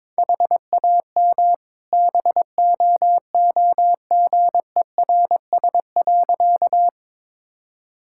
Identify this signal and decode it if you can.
Morse code, heard as a keyed tone, about 21 wpm, 700 Hz. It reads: HAM BOOGERS.
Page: morse-code-ninja-practice-22wpm